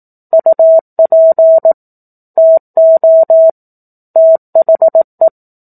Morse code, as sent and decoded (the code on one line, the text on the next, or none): ..- .--. / - --- / - .... .
UP TO THE